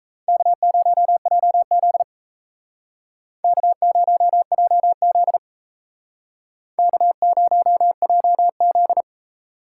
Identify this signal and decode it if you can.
Morse code, heard as a keyed tone, about 35 wpm, 700 Hz. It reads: K0JZ K0JZ K0JZ